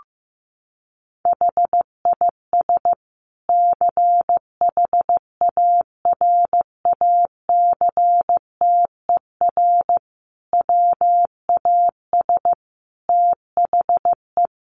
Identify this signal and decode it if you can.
Morse code, heard as a keyed tone, about 15 wpm, 700 Hz. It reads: HIS CHARACTER WAS THE
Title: Sentences from Top 1300 Words 15wpm